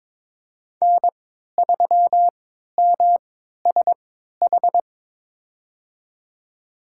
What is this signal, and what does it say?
Morse code, heard as a keyed tone, about 22 wpm, spaced as for 13 wpm, 700 Hz. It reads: N3MSH
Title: USA Call Signs - 5 Characters - 3x Character Spacing - 22wpm